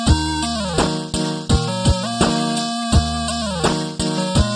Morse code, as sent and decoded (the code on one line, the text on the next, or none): none